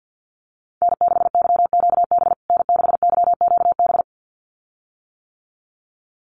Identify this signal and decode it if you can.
Morse code, heard as a keyed tone, about 50 wpm, 700 Hz. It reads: N6YQB N6YQB
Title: USA Call Signs Repeated 2x 50wpm